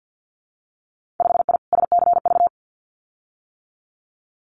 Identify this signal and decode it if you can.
Morse code, heard as a keyed tone, about 50 wpm, 700 Hz. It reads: 5I SCV
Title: Winter Field Day Exchanges Encore 50wpm